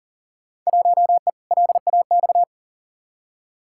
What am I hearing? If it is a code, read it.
Morse code, heard as a keyed tone, about 40 wpm, 700 Hz. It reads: 1E LAX
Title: Field Day Exchanges - 40wpm